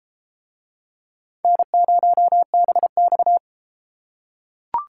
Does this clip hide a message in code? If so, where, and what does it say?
Yes: Morse code from 1.4 to 3.4 s, heard as a keyed tone, about 32 wpm, 700 Hz. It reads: N0BX